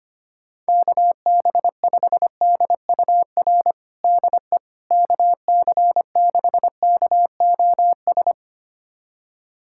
Morse code, heard as a keyed tone, about 25 wpm, 700 Hz. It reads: KB5DUR DE KC6KOS
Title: USA Callsign DE Callsign - 6 Characters - 25wpm